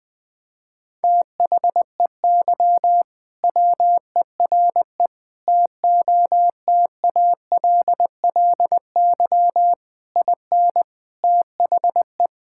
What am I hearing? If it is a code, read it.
Morse code, heard as a keyed tone, about 20 wpm, 700 Hz. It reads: THEY WERE TOTALLY IN THE
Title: Sentences from Top 1600 Words 20wpm